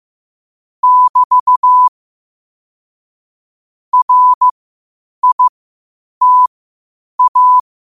Morse code, heard as a keyed tone, about 15 wpm, spaced as for 8 wpm, 1000 Hz. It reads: <BT> RITA
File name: qotdfiles / 8wpm